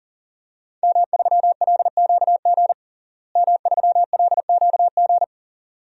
Morse code, heard as a keyed tone, about 37 wpm, 700 Hz. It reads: M3LQG M3LQG
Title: DXCC-100 Call Signs - 5 Characters - Repeated 2x - 40wpm